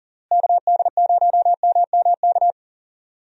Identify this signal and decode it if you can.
Morse code, heard as a keyed tone, about 40 wpm, 700 Hz. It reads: KD0MMK
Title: USA Call Signs - 6 Characters - Repeated 3x - 40wpm